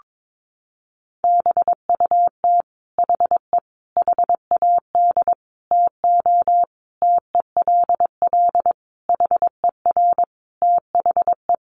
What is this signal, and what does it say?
Morse code, heard as a keyed tone, about 22 wpm, 700 Hz. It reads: BUT HE HAD TO TELL HER THE
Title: Sentences from Top 800 Words 22wpm